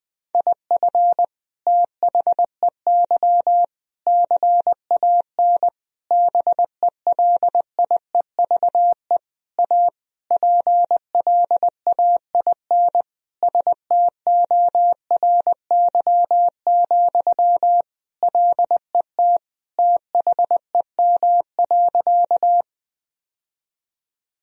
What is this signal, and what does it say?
Morse code, heard as a keyed tone, about 20 wpm, 700 Hz. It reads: IF THEY CAN BELIEVE A PLAIN STORY, LET THEM.
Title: Sentences from Top 1000 Words - Review Part 2 - 20wpm